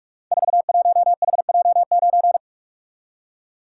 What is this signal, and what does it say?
Morse code, heard as a keyed tone, about 44 wpm, 700 Hz. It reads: 41HJ9
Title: Code Groups - 5 Letter:Numbers - 45wpm